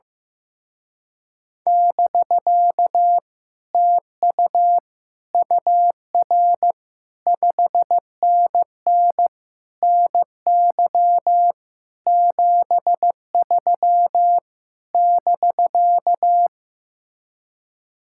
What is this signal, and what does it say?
Morse code, heard as a keyed tone, about 15 wpm, 700 Hz. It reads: <BK> TU UR 5NN NY 73 <BK>
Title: Speed-Racing Common QSO Phrases Encore - 15wpm